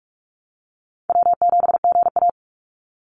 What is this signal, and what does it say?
Morse code, heard as a keyed tone, about 46 wpm, 690 Hz. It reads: W7GA
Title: NA SOTA Chaser Callsigns 45wpm